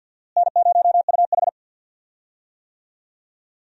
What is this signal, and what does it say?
Morse code, heard as a keyed tone, about 46 wpm, 700 Hz. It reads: N0UH
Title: USA Call Signs - 4 Characters - Repeated 2x - 50wpm